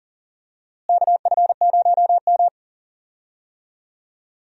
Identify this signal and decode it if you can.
Morse code, heard as a keyed tone, about 40 wpm, 700 Hz. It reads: KF0M
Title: USA Call Signs - 4 Characters - Repeated 2x - 40wpm